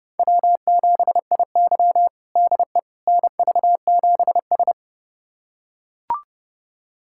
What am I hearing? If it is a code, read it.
Morse code, heard as a keyed tone, about 31 wpm, 700 Hz. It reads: W7IY DE NV7S
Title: USA Callsign DE Callsign - 4 Characters - 30wpm